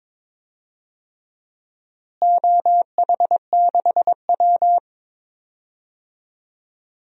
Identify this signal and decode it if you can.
Morse code, heard as a keyed tone, about 22 wpm, 700 Hz. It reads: OH6W
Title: DXCC-100 Call Signs - 4 Characters - Repeated 3x - 22wpm